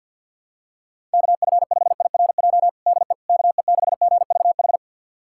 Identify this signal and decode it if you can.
Morse code, heard as a keyed tone, about 52 wpm, 690 Hz. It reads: KF5IRJ DE KE6GVH
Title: USA Callsign DE Callsign - 6 Characters - 50wpm